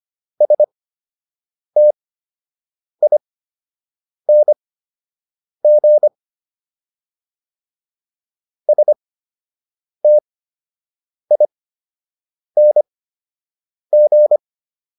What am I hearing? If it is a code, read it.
Morse code, heard as a keyed tone, about 25 wpm, spaced as for 7 wpm, 600 Hz. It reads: STING STING